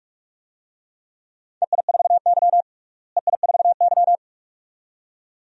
Morse code, heard as a keyed tone, about 42 wpm, 700 Hz. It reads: EI4Y EI4Y
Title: DXCC-100 Call Signs - 4 Characters - Repeated 2x - 45wpm